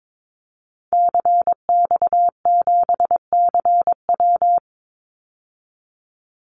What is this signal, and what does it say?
Morse code, heard as a keyed tone, about 22 wpm, 700 Hz. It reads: CX7CW
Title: Call Signs Repeated 3x - 22wpm